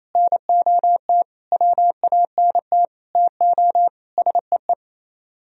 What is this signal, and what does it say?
Morse code, heard as a keyed tone, about 37 wpm, 700 Hz. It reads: NOT WANT TO SEE